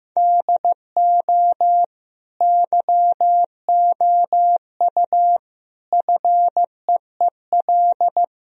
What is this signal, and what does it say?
Morse code, heard as a keyed tone, about 15 wpm, 700 Hz. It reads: DO YOU FEEL
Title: Sentences from Top 1000 Words - Review Part 3 - 15wpm